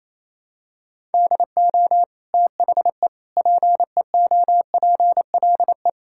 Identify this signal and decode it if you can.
Morse code, heard as a keyed tone, about 28 wpm, 700 Hz. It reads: DO THE PEOPLE
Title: Sentences from Top 1000 Words - Review Part 5 - 28wpm